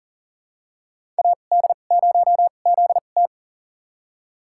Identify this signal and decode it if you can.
Morse code, heard as a keyed tone, about 38 wpm, spaced as for 28 wpm, 700 Hz. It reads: AD0ZT
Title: USA Call Signs - 5 Characters - 2x Character Spacing - 40wpm